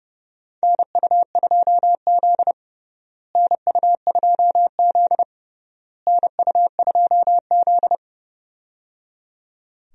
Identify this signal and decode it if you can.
Morse code, heard as a keyed tone, about 30 wpm, 700 Hz. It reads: NU2Z NU2Z NU2Z
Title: USA Call Signs - 4 Characters - Repeated 3x - 30wpm